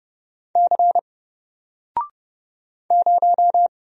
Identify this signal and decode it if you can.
Morse code, heard as a keyed tone, about 30 wpm, 700 Hz. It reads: C 0
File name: Single-Letter-Number-30wpm